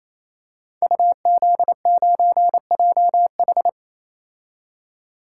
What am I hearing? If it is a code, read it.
Morse code, heard as a keyed tone, about 27 wpm, 700 Hz. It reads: UZ9JH
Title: Code Groups - 5 Letter:Numbers - 28wpm